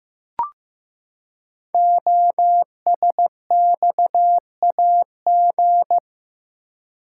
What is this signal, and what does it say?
Morse code, heard as a keyed tone, about 15 wpm, 700 Hz. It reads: OSXAG
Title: Code Groups - 5 Letters - 15wpm